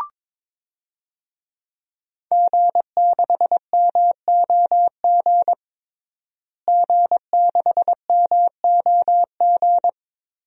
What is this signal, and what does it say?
Morse code, heard as a keyed tone, about 22 wpm, 700 Hz. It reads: G6MOG G6MOG
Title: DXCC-100 Call Signs - 5 Characters - Repeated 3x - 22wpm